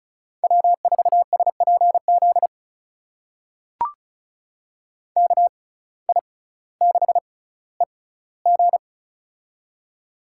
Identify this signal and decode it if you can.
Morse code, heard as a keyed tone, about 36 wpm, spaced as for 18 wpm, 700 Hz. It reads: W4SPZ KI6EG
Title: USA Call Signs - 5 Characters - 6x Character Spacing - 35wpm